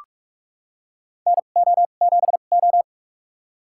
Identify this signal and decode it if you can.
Morse code, heard as a keyed tone, about 46 wpm, spaced as for 32 wpm, 700 Hz. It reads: NO7O
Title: USA Call Signs - 4 Characters - 2x Character Spacing - 45wpm